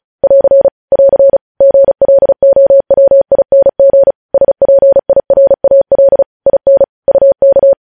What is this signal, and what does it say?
Morse code, heard as a keyed tone, about 36 wpm, 550 Hz. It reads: <AR> <AR> GLOWING SPIRAL IN UK
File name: podcast / News headlines-1x-35x35